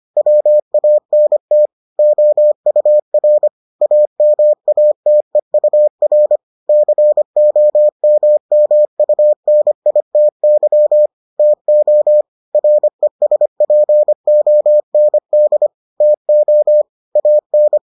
Morse code, heard as a keyed tone, about 25 wpm, 600 Hz. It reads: WANT OUR AMATEUR COMMUNITY TO RESPOND TO AN